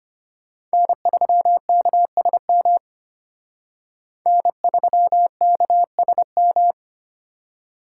Morse code, heard as a keyed tone, about 32 wpm, 700 Hz. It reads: N3KSM N3KSM